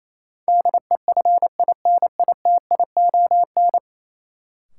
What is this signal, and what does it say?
Morse code, heard as a keyed tone, about 28 wpm, 700 Hz. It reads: DEFINITION
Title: Sentences from Top 1000 Words - Review Part 4 - New Word Drill - 28wpm